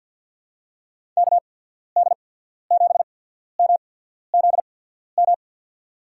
Morse code, heard as a keyed tone, about 46 wpm, spaced as for 13 wpm, 700 Hz. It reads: KD7MZM
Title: USA Call Signs - 8x Character Spacing - 50wpm